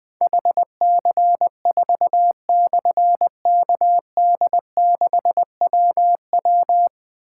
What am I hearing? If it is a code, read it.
Morse code, heard as a keyed tone, about 21 wpm, 700 Hz. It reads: HC4/KD6WW
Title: Difficult Call Signs 20wpm